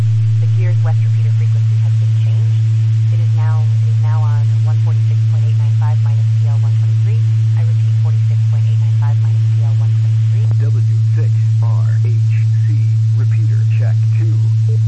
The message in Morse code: none

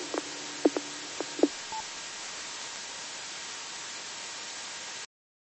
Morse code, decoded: none

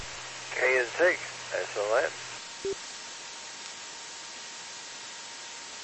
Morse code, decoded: none

